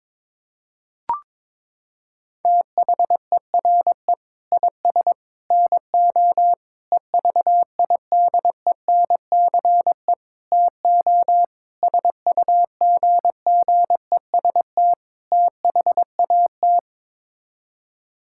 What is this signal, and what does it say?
Morse code, heard as a keyed tone, about 21 wpm, 700 Hz. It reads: THERE IS NO EVIDENCE TO SUGGEST THAT
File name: practice / 22wpm / Sets of 7 Words - 22wpm